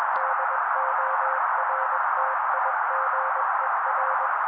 Morse code, heard as a keyed tone, about 21 wpm, 530 Hz. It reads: DORTIGER
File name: daily / 20wpm noise